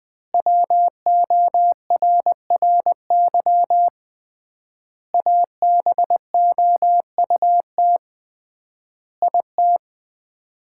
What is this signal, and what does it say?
Morse code, heard as a keyed tone, about 20 wpm, 700 Hz. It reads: WORRY ABOUT IT
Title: 3x Word-Spacing - Sets of 5 Words - 20wpm